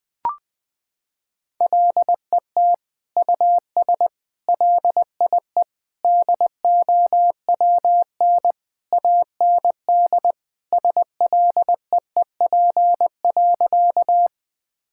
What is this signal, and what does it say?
Morse code, heard as a keyed tone, about 21 wpm, 700 Hz. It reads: LET US LIE DOWN AND SLEEP.
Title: Sentences from Top 1600 Words 20wpm